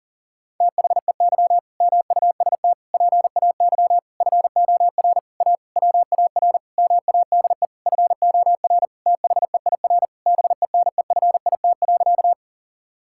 Morse code, heard as a keyed tone, about 40 wpm, 700 Hz. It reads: THEY MUST PAY FOR A WAR MADE FOR THEIR BENEFIT.
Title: Sentences from Top 1000 Words - Review Part 6 - 40wpm